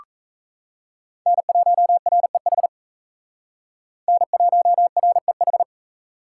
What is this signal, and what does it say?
Morse code, heard as a keyed tone, about 37 wpm, 700 Hz. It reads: N1REH N1REH